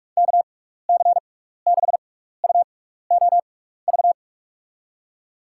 Morse code, heard as a keyed tone, about 44 wpm, spaced as for 15 wpm, 700 Hz. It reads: KC6UOV
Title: USA Call Signs - 6 Characters - 6x Character Spacing - 45wpm